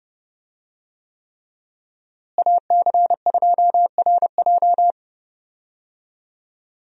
Morse code, heard as a keyed tone, about 30 wpm, 700 Hz. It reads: AC2RJ
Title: USA Call Signs - 5 Characters - Repeated 3x - 30wpm